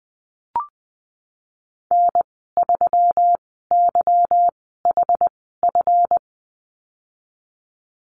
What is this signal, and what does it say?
Morse code, heard as a keyed tone, about 20 wpm, spaced as for 15 wpm, 700 Hz. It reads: N3YHF
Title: 173 Lesson 28g - Number 9 - Call Signs 20wpm